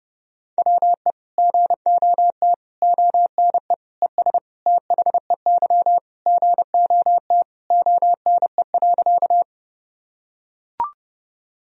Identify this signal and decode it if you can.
Morse code, heard as a keyed tone, about 30 wpm, 700 Hz. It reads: WE GOT ONE ES THEY GOT ONE.
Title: Sentences from Top 300 Words and Abbreviations 30wpm